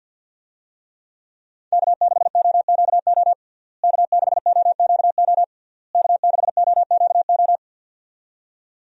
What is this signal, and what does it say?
Morse code, heard as a keyed tone, about 52 wpm, 700 Hz. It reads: K6OQO K6OQO K6OQO
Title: USA Call Signs - 5 Characters - Repeated 3x - 50wpm